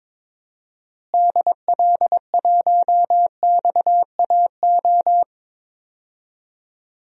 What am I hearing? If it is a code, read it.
Morse code, heard as a keyed tone, about 22 wpm, 700 Hz. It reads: DL1XAO
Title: DXCC-100 Call Signs - 6 Characters - Repeated 2x - 22wpm